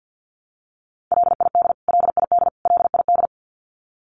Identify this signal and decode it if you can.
Morse code, heard as a keyed tone, about 50 wpm, 700 Hz. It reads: LID LID LID
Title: Abbreviations Repeated 3x - 50wpm